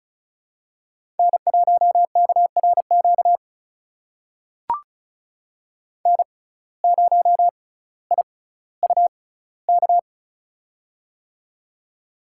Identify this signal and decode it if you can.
Morse code, heard as a keyed tone, about 34 wpm, spaced as for 18 wpm, 700 Hz. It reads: N1KRQ N0IUK
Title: USA Call Signs - 5 Characters - 6x Character Spacing - 35wpm